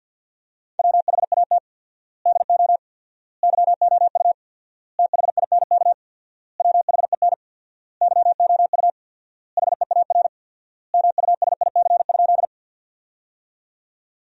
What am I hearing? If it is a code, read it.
Morse code, heard as a keyed tone, about 50 wpm, 700 Hz. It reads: WHAT DO YOU THINK WHEN YOU HEAR MUSIC?
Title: 4x Word-Spacing - Sentences from Top 300 Words - 50wpm